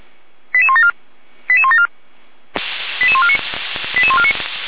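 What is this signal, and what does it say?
Five-tone selective call (ZVEI), heard as quick runs of five tones, tones of about 70 ms. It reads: 88174 88174 88149 88149
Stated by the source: Five-tone ZVEI selective call received on SDR.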